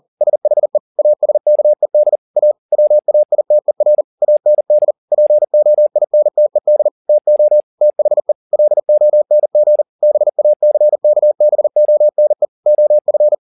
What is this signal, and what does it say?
Morse code, heard as a keyed tone, about 40 wpm, 600 Hz. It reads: SHE ASKED A WAITER AND POINTED TO THE LONG BACKBONE OF
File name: the old man and the sea 2 40wpm